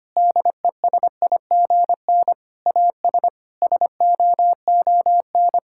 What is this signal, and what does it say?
Morse code, heard as a keyed tone, about 25 wpm, 700 Hz. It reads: DESIGN AS SOON
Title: Sentences from Top 1000 Words - Review Part 5 - 25wpm